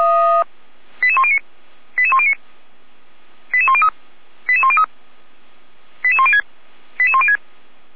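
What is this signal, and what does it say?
Five-tone selective call (ZVEI), heard as quick runs of five tones, tones of about 70 ms. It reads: 88109 88109 88103 88103 88106 88106
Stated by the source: Five-tone ZVEI selective call received on SDR, code 88192.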